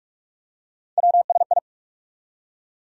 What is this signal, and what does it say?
Morse code, heard as a keyed tone, about 45 wpm, 700 Hz. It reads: WSI
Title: USA Call Sign Postfixes - 3 Letters 45wpm